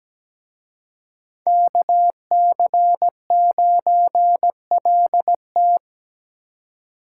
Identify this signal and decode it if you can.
Morse code, heard as a keyed tone, about 20 wpm, 700 Hz. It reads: KC9LT
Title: Speed Racing - USA 2x2 Call Signs 17wpm